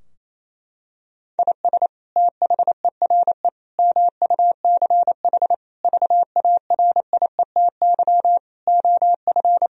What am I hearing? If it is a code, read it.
Morse code, heard as a keyed tone, about 28 wpm, 700 Hz. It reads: IS THERE MUCH VARIETY OF
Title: Sentences from Top 1000 Words - Review Part 4 - 28wpm